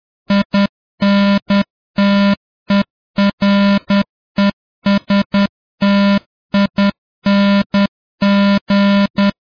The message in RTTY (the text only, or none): none